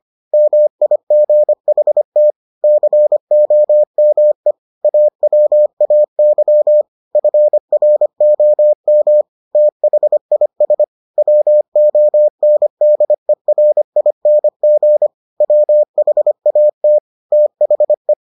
Morse code, heard as a keyed tone, about 25 wpm, 600 Hz. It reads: MIGHT COME AWAY FROM THIS WONDERING WHAT THE